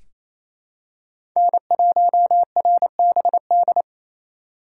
Morse code, heard as a keyed tone, about 28 wpm, 700 Hz. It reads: N1RBD
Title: K1USN SST QSO Practice - 28wpm